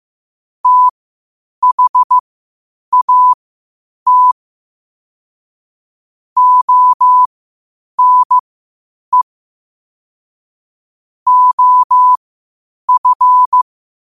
Morse code, heard as a keyed tone, about 15 wpm, spaced as for 8 wpm, 1000 Hz. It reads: THAT ONE OF
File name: qotdfiles / 8wpm